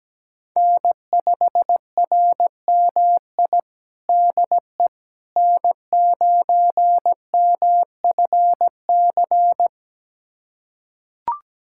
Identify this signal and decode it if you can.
Morse code, heard as a keyed tone, about 17 wpm, 700 Hz. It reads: N5RMI DE N9MFC
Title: USA Callsign DE Callsign - 5 Characters - 17wpm